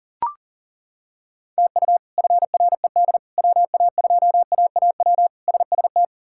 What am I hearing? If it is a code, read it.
Morse code, heard as a keyed tone, about 40 wpm, 700 Hz. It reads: TU FRED WA2AAW SST
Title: K1USN SST QSO Practice - 40wpm